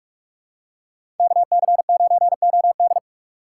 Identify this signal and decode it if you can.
Morse code, heard as a keyed tone, about 48 wpm, 700 Hz. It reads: KC9OD
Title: USA Call Signs - 5 Characters - 6x Character Spacing - 45wpm